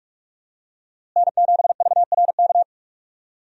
Morse code, heard as a keyed tone, about 43 wpm, 700 Hz. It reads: N7VRK
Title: USA Call Signs - 5 Characters - Repeated 2x - 45wpm